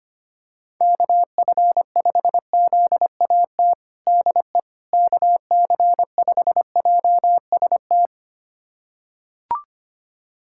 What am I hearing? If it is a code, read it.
Morse code, heard as a keyed tone, about 25 wpm, 700 Hz. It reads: KF5ZAT DE KC5JST
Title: USA Callsign DE Callsign - 6 Characters - 25wpm